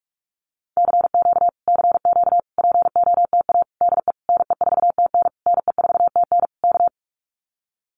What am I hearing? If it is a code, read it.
Morse code, heard as a keyed tone, about 44 wpm, 700 Hz. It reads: CQ CQ POTA DE NE4TN NE4TN K
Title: Common QSO Phrases Encore 45wpm